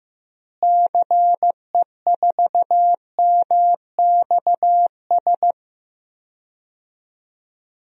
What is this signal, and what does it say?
Morse code, heard as a keyed tone, about 15 wpm, 700 Hz. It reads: CE4MXS
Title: DXCC-100 Call Signs - 6 Characters - Repeated 3x - 15wpm